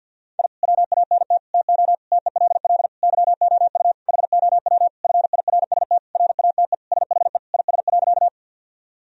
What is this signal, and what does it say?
Morse code, heard as a keyed tone, about 50 wpm, 700 Hz. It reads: I WANT TO TELL YOU HOW FIRST RATE SHE IS.
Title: Sentences from Top 1000 Words - Review Part 4 - 50wpm